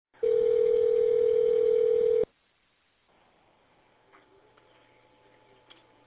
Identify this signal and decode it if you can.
Ringback tone (North America), 440 + 480 Hz.